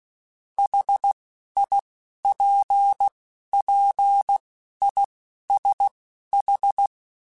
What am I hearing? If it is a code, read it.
Morse code, heard as a keyed tone, about 16 wpm, spaced as for 11 wpm, 800 Hz. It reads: HIPPISH
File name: General / General-mp3s / ISH @10